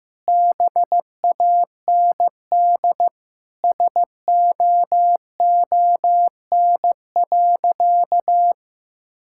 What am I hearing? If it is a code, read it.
Morse code, heard as a keyed tone, about 15 wpm, 700 Hz. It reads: BAND SOON.
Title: Sentences from Top 1600 Words 15wpm